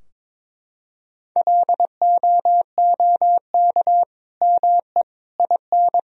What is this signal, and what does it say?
Morse code, heard as a keyed tone, about 22 wpm, 700 Hz. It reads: LOOK ME IN
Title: Sentences from Top 1000 Words - Review Part 2 - 22wpm